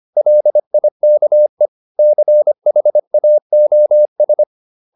Morse code, heard as a keyed tone, about 25 wpm, 600 Hz. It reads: LIKE CHAOS